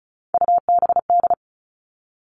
Morse code, heard as a keyed tone, about 34 wpm, 700 Hz. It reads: UBD
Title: All Cut Numbers - 3-Digit 35wpm